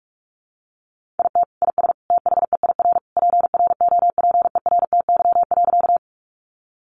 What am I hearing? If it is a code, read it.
Morse code, heard as a keyed tone, about 44 wpm, 700 Hz. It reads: IT IS THEIR PROPERTY.
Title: Sentences from Top 1000 Words 45wpm